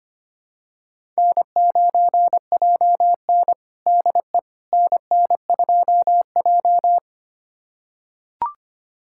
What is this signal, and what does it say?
Morse code, heard as a keyed tone, about 25 wpm, 700 Hz. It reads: N9JN DE NN2J
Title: USA Callsign DE Callsign - 4 Characters - 25wpm